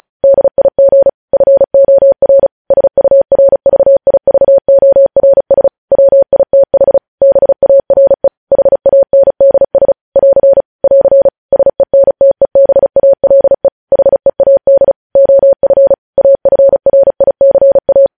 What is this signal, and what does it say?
Morse code, heard as a keyed tone, about 36 wpm, 550 Hz. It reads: DIG FOR SURVIVORS WITH BARE HANDS <AR> <AR> SENTEBALE HEAD OF AFRICA
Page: News headlines in Morse code at 35 wpm